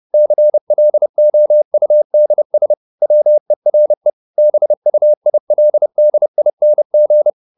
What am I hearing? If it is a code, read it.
Morse code, heard as a keyed tone, about 30 wpm, 600 Hz. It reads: CLOUDS WERE BUILDING
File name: the old man and the sea / the old man and the sea 1 30wpm